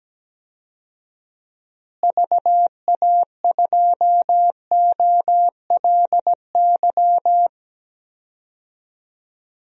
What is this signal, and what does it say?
Morse code, heard as a keyed tone, about 17 wpm, 700 Hz. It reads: VA2OLY
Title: DXCC-100 Call Signs - 6 Characters - Repeated 2x - 17wpm